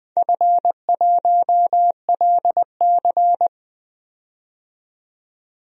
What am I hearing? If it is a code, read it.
Morse code, heard as a keyed tone, about 20 wpm, 700 Hz. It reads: F1LC
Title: DXCC-100 Call Signs - 4 Characters - 20wpm